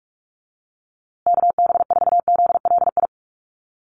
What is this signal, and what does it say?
Morse code, heard as a keyed tone, about 44 wpm, 700 Hz. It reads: KB4ZLI
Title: USA Call Signs Repeated 3x 45wpm